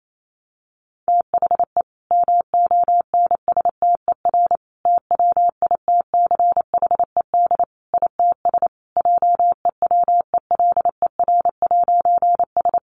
Morse code, heard as a keyed tone, about 28 wpm, 700 Hz. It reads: THE MONSTER TWITCHED ITS JEWELER'S
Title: A Sound Of Thunder 28wpm